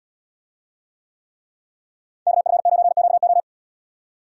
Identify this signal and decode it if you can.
Morse code, heard as a keyed tone, about 78 wpm, 700 Hz. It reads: KF9QO